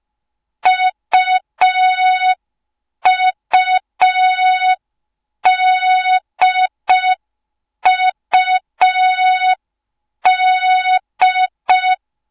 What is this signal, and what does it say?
Morse code, heard as a keyed tone, about 5 wpm, 3700 Hz. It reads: UUDUD